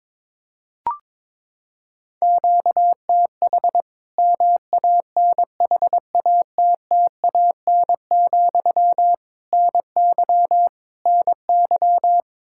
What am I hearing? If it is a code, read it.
Morse code, heard as a keyed tone, about 21 wpm, 700 Hz. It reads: QTH MANHATTAN, NY NY
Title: Practice QSO - Exchange 1 - Beginning - 22wpm